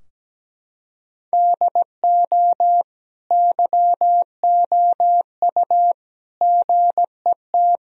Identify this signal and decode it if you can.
Morse code, heard as a keyed tone, about 17 wpm, 700 Hz. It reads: DO YOU GET
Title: Sentences from Top 1000 Words - Review Part 4 - 17wpm